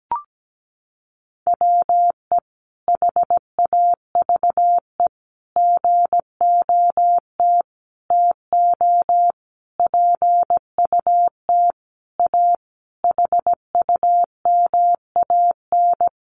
Morse code, heard as a keyed tone, about 17 wpm, 700 Hz. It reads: WE HAVE GOT TO PUT A HUMAN
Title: Sentences from Top 1200 Words 17wpm